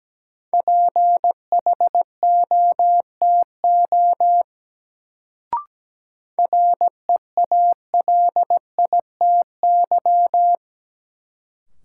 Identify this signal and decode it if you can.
Morse code, heard as a keyed tone, about 17 wpm, 700 Hz. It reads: PHOTO REALITY
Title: Sentences from Top 1200 Words - New Word Drill - 17wpm